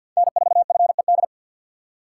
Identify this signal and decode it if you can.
Morse code, heard as a keyed tone, about 62 wpm, 700 Hz. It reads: N4FED
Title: Speed Racing - USA Call Signs 50wpm